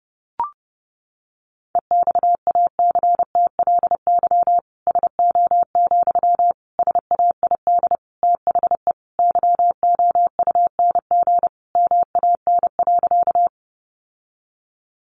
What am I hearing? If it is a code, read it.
Morse code, heard as a keyed tone, about 30 wpm, 700 Hz. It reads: EXACTLY SO, SAID THE YOUNG MAN.
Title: Sentences from Top 900 Words 30wpm